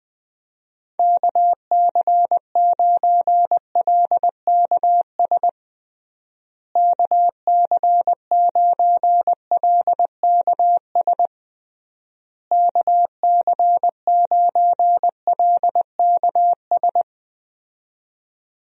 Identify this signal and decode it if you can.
Morse code, heard as a keyed tone, about 20 wpm, 700 Hz. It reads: KC9LKS KC9LKS KC9LKS
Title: USA Call Signs - 6 Characters - Repeated 3x - 20wpm